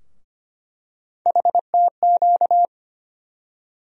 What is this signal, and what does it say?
Morse code, heard as a keyed tone, about 25 wpm, 700 Hz. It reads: HTQ
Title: USA Call Sign Postfixes - 3 Letters 25wpm